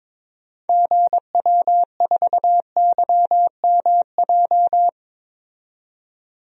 Morse code, heard as a keyed tone, about 22 wpm, 700 Hz. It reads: GW4YMJ
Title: DXCC-100 Call Signs - 6 Characters - 22wpm